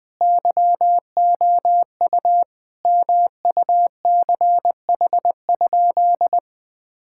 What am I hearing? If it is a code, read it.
Morse code, heard as a keyed tone, about 21 wpm, 700 Hz. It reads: YOU MUCH?
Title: Sentences from Top 1000 Words - Review Part 6 - 20wpm